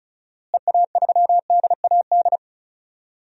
Morse code, heard as a keyed tone, about 36 wpm, 700 Hz. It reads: EA3DAD
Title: DXCC-100 Call Signs - 6 Characters - Repeated 3x - 35wpm